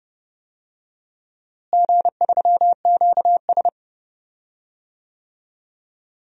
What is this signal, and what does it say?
Morse code, heard as a keyed tone, about 31 wpm, 700 Hz. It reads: G3QS
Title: DXCC-100 Call Signs - 4 Characters - Repeated 3x - 30wpm